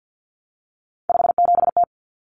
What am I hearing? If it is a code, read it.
Morse code, heard as a keyed tone, about 50 wpm, 700 Hz. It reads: 57T
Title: Common Cut Numbers - 3-Digit 50wpm